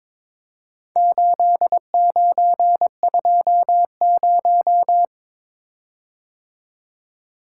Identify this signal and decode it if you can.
Morse code, heard as a keyed tone, about 22 wpm, 700 Hz. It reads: 8920
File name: practice / 22wpm / Four Letter-Number Combinations - 22wpm